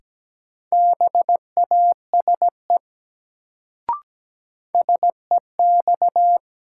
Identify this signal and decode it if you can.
Morse code, heard as a keyed tone, about 17 wpm, 700 Hz. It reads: BASE SEX
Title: Sentences from Top 1600 Words - New Word Drill Alt - 17wpm